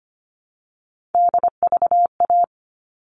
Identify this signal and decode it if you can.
Morse code, heard as a keyed tone, about 25 wpm, 700 Hz. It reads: DVA
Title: All Cut Numbers - 3-Digit 25wpm